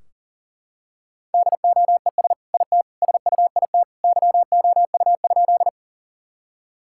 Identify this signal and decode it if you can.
Morse code, heard as a keyed tone, about 40 wpm, 700 Hz. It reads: DOES IT SUIT YOU?
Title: Sentences from Top 1600 Words 40wpm